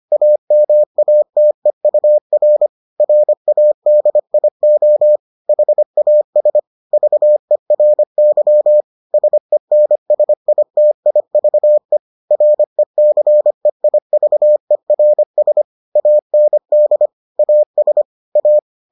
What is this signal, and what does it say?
Morse code, heard as a keyed tone, about 25 wpm, 600 Hz. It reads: AMATEUR RADIO HAS VERY SENSITIVE RECEIVERS AND AS A